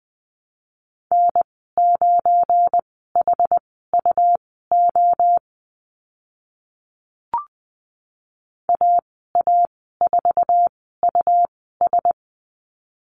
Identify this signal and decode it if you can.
Morse code, heard as a keyed tone, about 20 wpm, spaced as for 14 wpm, 700 Hz. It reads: N9HUO AA4US
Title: 207 Lesson 33g - Letter j - Call Signs 20wpm